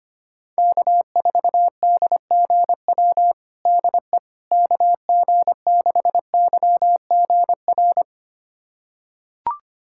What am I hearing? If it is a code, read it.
Morse code, heard as a keyed tone, about 25 wpm, 700 Hz. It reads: K4DGW DE KG6YGR
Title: USA Callsign DE Callsign 25wpm